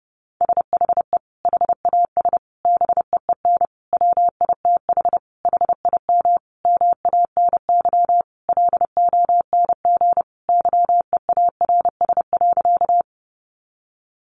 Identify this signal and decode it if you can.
Morse code, heard as a keyed tone, about 30 wpm, 700 Hz. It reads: SHE HAS BEEN WITH HIM MANY LONG YEARS.
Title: Sentences from Top 200 Words 30wpm